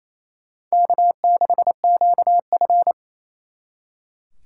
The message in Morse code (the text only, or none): K6QF